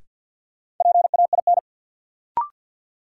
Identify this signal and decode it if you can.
Morse code, heard as a keyed tone, about 53 wpm, 700 Hz. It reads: PAIN
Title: Sentences from Top 1600 Words - New Word Drill - 50wpm